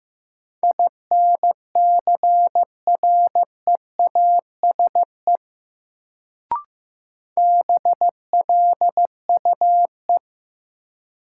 Morse code, heard as a keyed tone, about 15 wpm, 700 Hz. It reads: INCREASE BLUE
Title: Sentences from Top 1200 Words - New Word Drill Alt - 15wpm